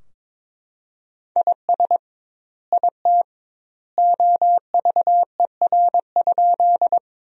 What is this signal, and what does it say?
Morse code, heard as a keyed tone, about 21 wpm, 700 Hz. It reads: IS IT OVER?
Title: 2x Word-Spacing - Sentences from Top 100 Words - 22wpm